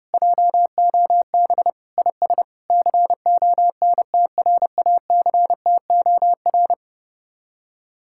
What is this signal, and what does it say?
Morse code, heard as a keyed tone, about 32 wpm, 700 Hz. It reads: JOB IS CONTRACTOR